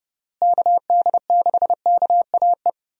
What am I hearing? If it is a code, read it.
Morse code, heard as a keyed tone, about 31 wpm, 700 Hz. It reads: KD6KAE
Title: USA Call Signs - 6 Characters - Repeated 2x - 30wpm